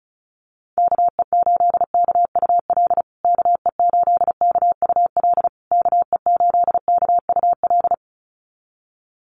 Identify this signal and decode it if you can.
Morse code, heard as a keyed tone, about 34 wpm, 700 Hz. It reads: KE8KUL KE8KUL KE8KUL
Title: USA Call Signs Repeated 3x 35wpm